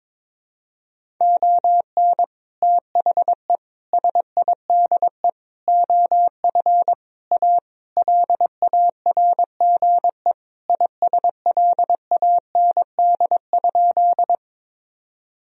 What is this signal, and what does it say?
Morse code, heard as a keyed tone, about 22 wpm, 700 Hz. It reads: ON THE SIDE OF A LARGE ISLAND?
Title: Sentences from Top 1000 Words - Review Part 3 - 22wpm